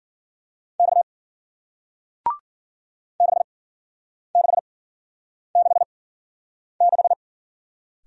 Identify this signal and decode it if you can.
Morse code, heard as a keyed tone, about 53 wpm, spaced as for 9 wpm, 700 Hz. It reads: X 6666